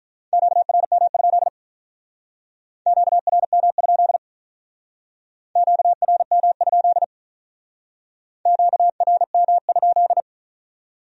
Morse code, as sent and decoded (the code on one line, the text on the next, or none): --.- .-. -- ..--.. / --.- .-. -- ..--.. / --.- .-. -- ..--.. / --.- .-. -- ..--..
QRM? QRM? QRM? QRM?